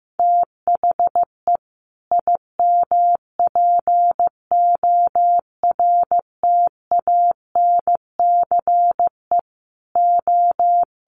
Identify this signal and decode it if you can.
Morse code, heard as a keyed tone, about 15 wpm, 700 Hz. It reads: THE IMPORTANCE O
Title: Sets of 3 Words 15wpm